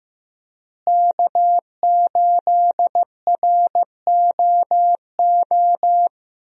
Morse code, heard as a keyed tone, about 15 wpm, 700 Hz. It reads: K8ROO
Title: USA Call Signs - 5 Characters - Repeated 3x - 15wpm